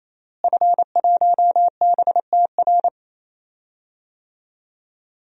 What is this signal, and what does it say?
Morse code, heard as a keyed tone, about 28 wpm, 700 Hz. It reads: F1BTR
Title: DXCC-100 Call Signs - 5 Characters - Repeated 3x - 28wpm